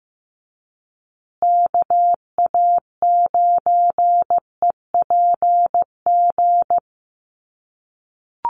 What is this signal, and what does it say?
Morse code, heard as a keyed tone, about 15 wpm, 700 Hz. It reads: KA9EPG
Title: USA Call Signs Repeated 3x 15wpm